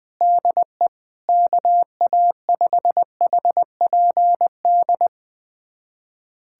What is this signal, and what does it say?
Morse code, heard as a keyed tone, about 21 wpm, 700 Hz. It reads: DE KA5HPD
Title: USA Callsign DE Callsign - 6 Characters - 20wpm